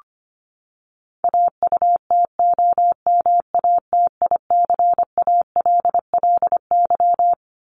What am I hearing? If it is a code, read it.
Morse code, heard as a keyed tone, about 25 wpm, 700 Hz. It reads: AUTOMATICALLY
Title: Top 1751-2000 Words 25wpm